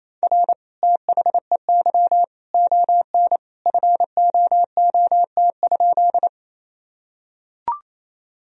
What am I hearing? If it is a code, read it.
Morse code, heard as a keyed tone, about 28 wpm, 700 Hz. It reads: R THEY ON FOOT?
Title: Sentences from Top 500 Words and Abbreviations 28wpm